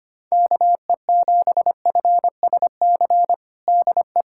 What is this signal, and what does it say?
Morse code, heard as a keyed tone, about 25 wpm, 700 Hz. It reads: KE7FSC DE
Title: USA Callsign DE Callsign - 6 Characters - 25wpm